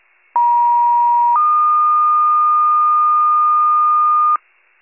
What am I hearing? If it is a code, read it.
Two-tone page, heard as a one-second tone, then a three-second tone, about 950 then 1220 Hz, with noise. It is heard from 0.3 to 4.4 s.